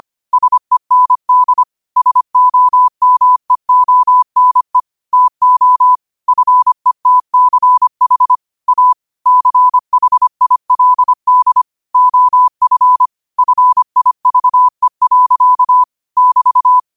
Morse code, heard as a keyed tone, about 25 wpm, 1000 Hz. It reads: SEND SOMEONE TO FETCH A CHILD OF FIVE. <BT>
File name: 25wpm